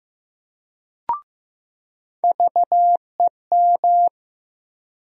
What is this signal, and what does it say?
Morse code, heard as a keyed tone, about 15 wpm, 700 Hz. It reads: VEM
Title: USA Call Sign Postfixes - 3 Letters 15wpm